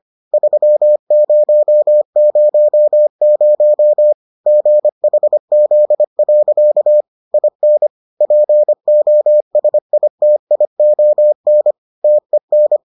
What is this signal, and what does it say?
Morse code, heard as a keyed tone, about 25 wpm, 600 Hz. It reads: 3000 GHZ. IN POSITION TEN